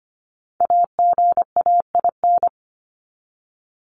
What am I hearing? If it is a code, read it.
Morse code, heard as a keyed tone, about 25 wpm, 700 Hz. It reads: AGAIN
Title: Top 200 Words 25wpm